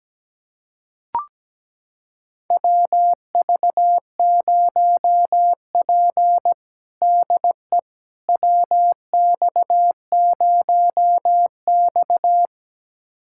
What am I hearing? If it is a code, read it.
Morse code, heard as a keyed tone, about 17 wpm, 700 Hz. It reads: WV0P DE WX0X
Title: USA Callsign DE Callsign - 4 Characters - 17wpm